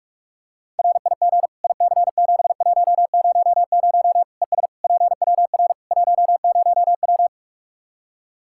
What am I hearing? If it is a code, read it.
Morse code, heard as a keyed tone, about 46 wpm, 700 Hz. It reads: RIG IC7100 ES PWR 10W
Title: K1USN SST QSO Practice 45wpm